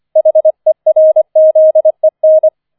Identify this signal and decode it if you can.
Morse code, heard as a keyed tone, about 25 wpm, 630 Hz. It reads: HERZEN
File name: 004 Vierter Streich 25wpm